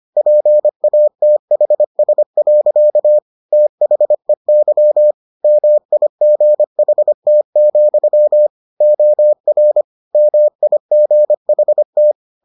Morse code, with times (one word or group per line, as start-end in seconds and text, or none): 0.2-3.2 PATHS.
3.5-5.1 THEY
5.4-8.5 MIGHT,
8.8-9.8 OR
10.1-12.1 MIGHT